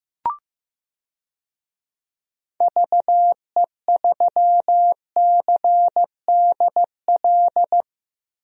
Morse code, heard as a keyed tone, about 15 wpm, 700 Hz. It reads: VE3CDL
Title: DXCC-100 Call Signs - 6 Characters - Repeated 2x - 15wpm